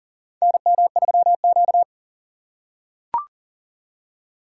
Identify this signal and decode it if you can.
Morse code, heard as a keyed tone, about 40 wpm, 700 Hz. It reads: NM3Q
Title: USA Call Signs - 4 Characters - 4x Character Spacing - 40wpm